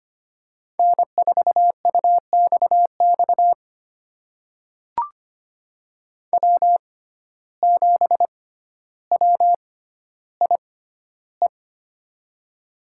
Morse code, heard as a keyed tone, about 25 wpm, spaced as for 13 wpm, 700 Hz. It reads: N4UXX W7WIE
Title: USA Call Signs - 5 Characters - 6x Character Spacing - 25wpm